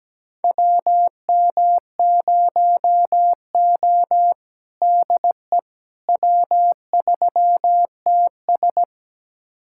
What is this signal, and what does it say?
Morse code, heard as a keyed tone, about 17 wpm, 700 Hz. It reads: WM0O DE W3TS
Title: USA Callsign DE Callsign - 4 Characters - 17wpm